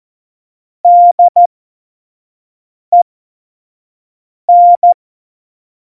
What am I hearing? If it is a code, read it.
Morse code, heard as a keyed tone, about 14 wpm, spaced as for 5 wpm, 700 Hz. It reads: DEN
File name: Morsekurs / Witze / Witz 07 April 2020, 25 F70